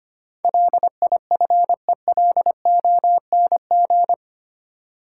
Morse code, heard as a keyed tone, about 25 wpm, 700 Hz. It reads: LIFELONG
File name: 8 Letter Words - First 3 Letters Given - 25wpm